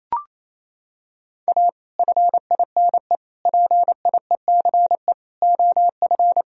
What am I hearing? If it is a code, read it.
Morse code, heard as a keyed tone, about 28 wpm, 700 Hz. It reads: A FINE PIECE OF
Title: Sentences from Top 1000 Words - Review Part 2 - 28wpm